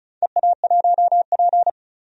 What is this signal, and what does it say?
Morse code, heard as a keyed tone, about 34 wpm, 700 Hz. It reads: EA1P
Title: DXCC-100 Call Signs - 4 Characters - 35wpm